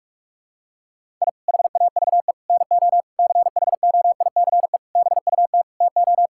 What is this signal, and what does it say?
Morse code, heard as a keyed tone, about 41 wpm, 700 Hz. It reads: I HAVE NO CHOICE BUT TO
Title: Sentences from Top 1000 Words - Review Part 6 - 45wpm